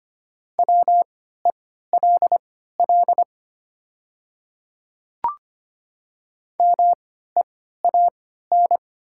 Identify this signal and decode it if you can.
Morse code, heard as a keyed tone, about 25 wpm, spaced as for 14 wpm, 700 Hz. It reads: WELL MEAN
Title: 3x Character Spacing - Top 500 Words - 25wpm